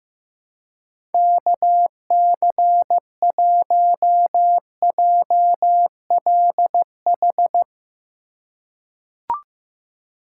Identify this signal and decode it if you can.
Morse code, heard as a keyed tone, about 15 wpm, 700 Hz. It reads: KC1JLH
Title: USA Call Signs - 6 Characters - 2x Character Spacing - 15wpm